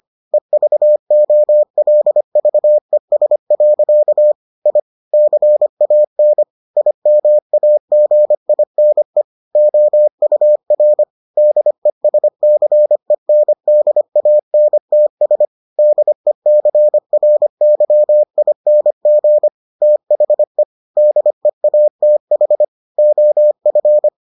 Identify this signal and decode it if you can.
Morse code, heard as a keyed tone, about 25 wpm, 600 Hz. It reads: EVOLVES. I CAN IMAGINE OUR DESCENDANTS DECRYING THE DEATH OF